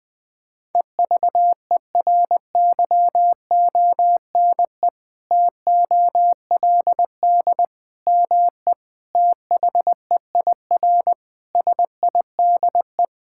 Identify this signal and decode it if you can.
Morse code, heard as a keyed tone, about 21 wpm, 700 Hz. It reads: EVERYONE TOLD ME THEIR SIDE
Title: Sentences from Top 1000 Words - Review Part 5 - 20wpm